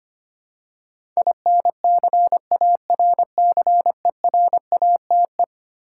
Morse code, heard as a keyed tone, about 25 wpm, 700 Hz. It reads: INCARCERATE
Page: morse-code-ninja-practice-25wpm